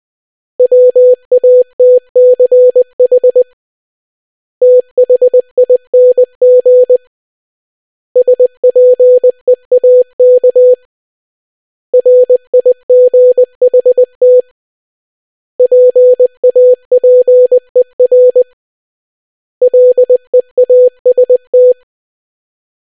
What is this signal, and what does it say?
Morse code, heard as a keyed tone, about 20 wpm, 500 Hz. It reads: WATCH THING SPEAK RIGHT PAPER LEAST